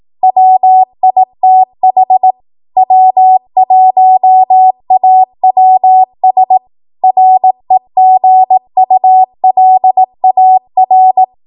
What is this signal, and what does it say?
Morse code, heard as a keyed tone, about 18 wpm, 750 Hz. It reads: WITH W1AWS REGULAR
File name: Archive / 18 WPM / 18WPM